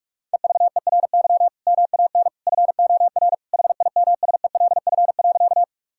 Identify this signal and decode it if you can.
Morse code, heard as a keyed tone, about 41 wpm, 700 Hz. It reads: EVERY MAN FOR HIMSELF.
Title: Sentences from Top 1600 Words 45wpm